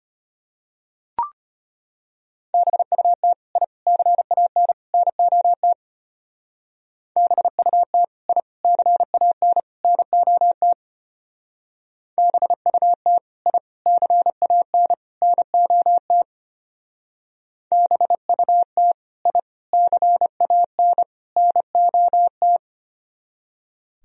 Morse code, heard as a keyed tone, about 32 wpm, 700 Hz. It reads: BUT I CAN NOT BUT I CAN NOT BUT I CAN NOT BUT I CAN NOT